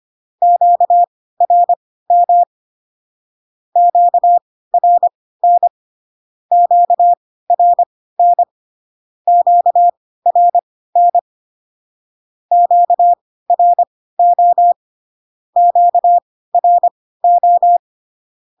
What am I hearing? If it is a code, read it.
Morse code, heard as a keyed tone, about 25 wpm, spaced as for 16 wpm, 700 Hz. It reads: QRM QRN QRN QRN QRO QRO
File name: cw / Q-codes most common repeated 3 times 10 25wpm ik1hge